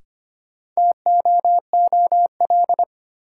Morse code, heard as a keyed tone, about 25 wpm, 700 Hz. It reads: TOOL
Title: Sentences from Top 1000 Words - Review Part 6 - New Word Drill - 25wpm